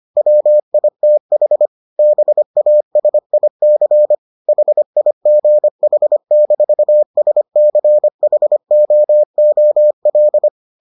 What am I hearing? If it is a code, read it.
Morse code, heard as a keyed tone, about 25 wpm, 600 Hz. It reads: WITH BASIC HIGH-SCHOOL